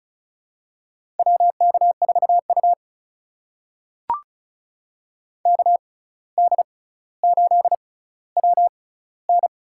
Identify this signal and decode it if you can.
Morse code, heard as a keyed tone, about 36 wpm, spaced as for 17 wpm, 700 Hz. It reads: WK4U KD8WN
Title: USA Call Signs - 6x Character Spacing - 35wpm